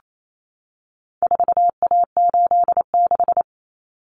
Morse code, heard as a keyed tone, about 28 wpm, 700 Hz. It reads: 4A86
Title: Common Cut Numbers - 4-Digit 28wpm